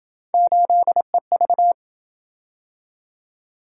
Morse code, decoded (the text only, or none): 8EV